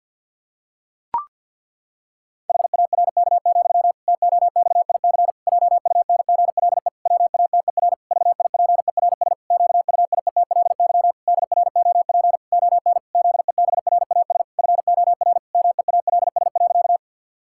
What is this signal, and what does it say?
Morse code, heard as a keyed tone, about 46 wpm, 700 Hz. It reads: HARK, TOXIC JUNGLE WATER VIPERS QUIETLY DROP ON ZEBRAS FOR MEALS.
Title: Pangram Challenge v2 - 50wpm